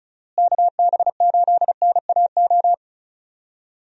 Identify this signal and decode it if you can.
Morse code, heard as a keyed tone, about 34 wpm, 700 Hz. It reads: KB8NAO
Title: USA Call Signs - 6 Characters - Repeated 2x - 35wpm